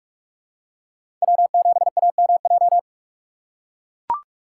Morse code, heard as a keyed tone, about 43 wpm, 700 Hz. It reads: W7AMJ
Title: USA Call Signs - 5 Characters - 4x Character Spacing - 45wpm